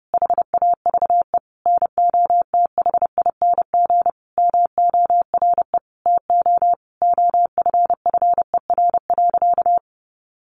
Morse code, heard as a keyed tone, about 30 wpm, 700 Hz. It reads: HAVE NOTHING MORE TO OFFER.
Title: Sentences from Top 1100 Words 30wpm